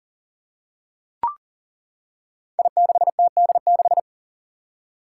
Morse code, heard as a keyed tone, about 39 wpm, 700 Hz. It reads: I6TD6
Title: Five Challenging Letter-Number Combinations 40wpm